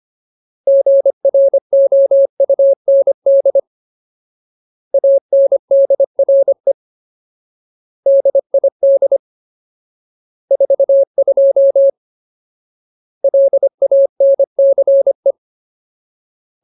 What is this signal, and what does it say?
Morse code, heard as a keyed tone, about 25 wpm, 550 Hz. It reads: GROUND ANDRE DID 42 LANCE
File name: Ultimate Word List / 25WPM Ultimate